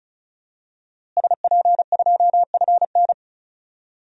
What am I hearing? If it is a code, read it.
Morse code, heard as a keyed tone, about 36 wpm, 700 Hz. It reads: SP2FN
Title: DXCC-100 Call Signs - 5 Characters - Repeated 2x - 35wpm